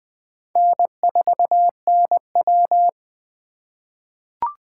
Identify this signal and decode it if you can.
Morse code, heard as a keyed tone, about 20 wpm, 700 Hz. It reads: N4NW
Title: K1USN SST QSO Practice - 20wpm